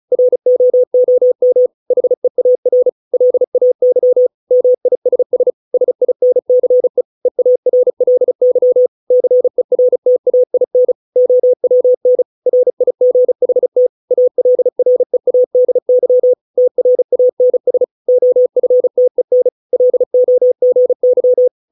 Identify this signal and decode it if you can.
Morse code, heard as a keyed tone, about 35 wpm, 500 Hz. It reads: ROOM HEAR LAY MISS SINCE EARLY CERTAIN OWN RIGHT ALREADY TRANS OFTEN LOGY